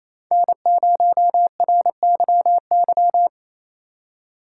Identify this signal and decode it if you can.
Morse code, heard as a keyed tone, about 27 wpm, 700 Hz. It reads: N0RYY
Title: USA Call Signs - 5 Characters - 8x Character Spacing - 28wpm